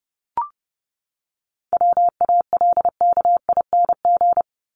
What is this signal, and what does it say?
Morse code, heard as a keyed tone, about 30 wpm, 700 Hz. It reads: WALKING
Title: ING Suffix - Top 100 Words with Base Words 30wpm